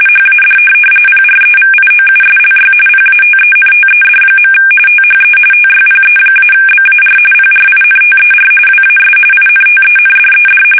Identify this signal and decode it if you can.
RTTY, about 45.45 baud. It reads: HAD PREVIOUSLY PATENTED. THE TYPE-BAR PRINTER WAS INTENDED
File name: fsk850 / teletype-corp